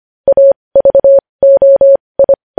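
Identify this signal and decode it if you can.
Morse code, heard as a keyed tone, about 25 wpm, spaced as for 20 wpm, 550 Hz. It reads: AVOI
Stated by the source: Morse news headlines sent at 20.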